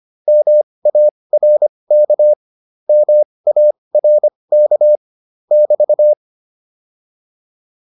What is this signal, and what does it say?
Morse code, heard as a keyed tone, about 25 wpm, spaced as for 20 wpm, 600 Hz. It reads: MARK MARK <BT>
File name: QSO210a 20